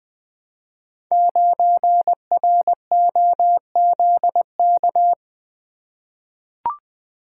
Morse code, heard as a keyed tone, about 20 wpm, 700 Hz. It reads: 9ROZK
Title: Code Groups - 5 Letter:Numbers - 20wpm